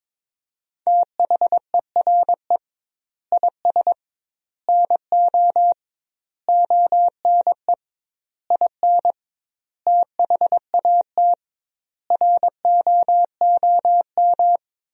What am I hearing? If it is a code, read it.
Morse code, heard as a keyed tone, about 22 wpm, 700 Hz. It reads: THERE IS NO ONE IN THAT ROOM
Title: 2x Word-Spacing - Sentences from Top 500 Words - 22wpm